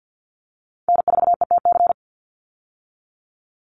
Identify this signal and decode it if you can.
Morse code, heard as a keyed tone, about 50 wpm, 700 Hz. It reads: N4ETC
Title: Call Signs - 50wpm